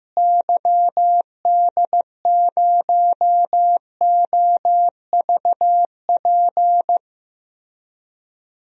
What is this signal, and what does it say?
Morse code, heard as a keyed tone, about 15 wpm, 700 Hz. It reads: YD0OVP
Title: DXCC-100 Call Signs - 6 Characters - Repeated 3x - 15wpm